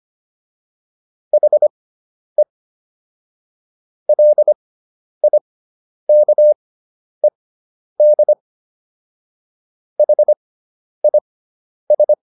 Morse code, heard as a keyed tone, about 25 wpm, spaced as for 10 wpm, 600 Hz. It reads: HE LIKED HIS